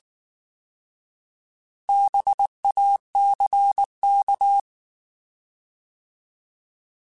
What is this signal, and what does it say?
Morse code, heard as a keyed tone, about 19 wpm, 800 Hz. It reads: BACK
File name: Alphabetical / Alphabetical mp3s / Bs @18